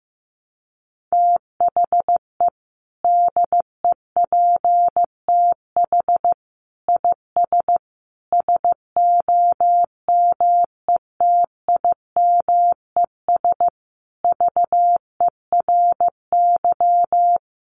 Morse code, heard as a keyed tone, about 14 wpm, 700 Hz. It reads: THE DEPTH IS SOMETIMES VERY
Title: Sentences from Top 1200 Words 15wpm